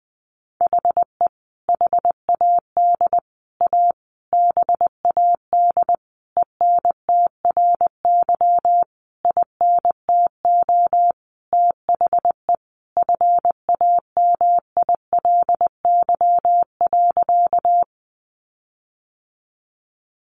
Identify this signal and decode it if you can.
Morse code, heard as a keyed tone, about 20 wpm, 700 Hz. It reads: HE HAD A BAD ENTRY INTO THE FAMILY.
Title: Sentences from Top 1500 Words 20wpm